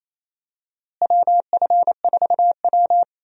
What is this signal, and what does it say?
Morse code, heard as a keyed tone, about 28 wpm, 700 Hz. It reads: WF4W
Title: USA Call Signs - 4 Characters - Repeated 2x - 28wpm